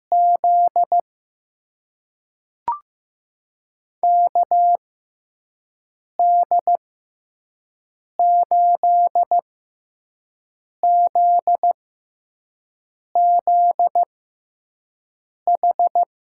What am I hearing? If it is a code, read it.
Morse code, heard as a keyed tone, about 15 wpm, spaced as for 5 wpm, 700 Hz. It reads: Z KD8ZZH
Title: USA Call Signs - 6 Characters - 6x Character Spacing - 15wpm